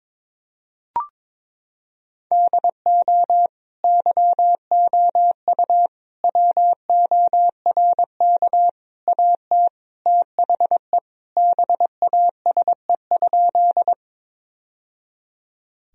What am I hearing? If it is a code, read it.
Morse code, heard as a keyed tone, about 21 wpm, 700 Hz. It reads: DO YOU WORK AT THE BASE?
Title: Sentences from Top 1000 Words - Review Part 1 - 22wpm